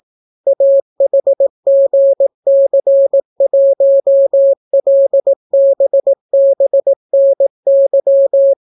Morse code, heard as a keyed tone, about 18 wpm, 550 Hz. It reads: AHGC1LBBNY